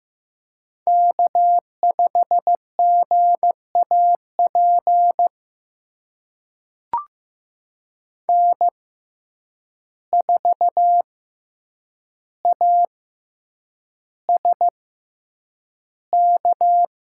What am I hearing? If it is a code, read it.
Morse code, heard as a keyed tone, about 15 wpm, spaced as for 8 wpm, 700 Hz. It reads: K5GAP N4ASK
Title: USA Call Signs - 5 Characters - 6x Character Spacing - 15wpm